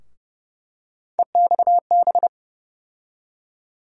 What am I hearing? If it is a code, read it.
Morse code, heard as a keyed tone, about 29 wpm, 700 Hz. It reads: EXB
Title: USA Call Sign Postfixes - 3 Letters 30wpm